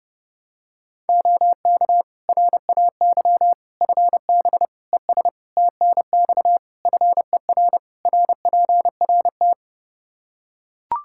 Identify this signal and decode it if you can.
Morse code, heard as a keyed tone, about 32 wpm, 700 Hz. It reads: OK RAY FB ES TNX FER RPRT
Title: Practice QSO - Exchange 2 - Info - 30wpm